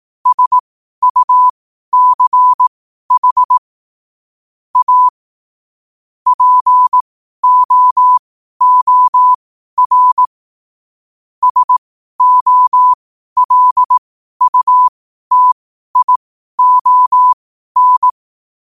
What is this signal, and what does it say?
Morse code, heard as a keyed tone, about 18 wpm, spaced as for 12 wpm, 1000 Hz. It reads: SUCH A POOR SOLUTION
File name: 12wpm